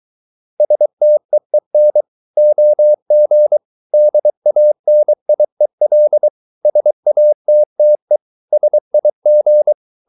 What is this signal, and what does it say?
Morse code, heard as a keyed tone, about 23 wpm, 600 Hz. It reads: STEEN OG DANIEL SATTE SIG
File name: Ferie QRS 23wpm